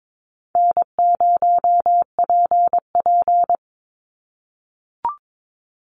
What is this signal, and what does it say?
Morse code, heard as a keyed tone, about 22 wpm, 700 Hz. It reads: N0PP
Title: Easy Call Signs - 22wpm